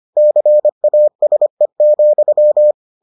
Morse code, heard as a keyed tone, about 25 wpm, 600 Hz. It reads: CASE,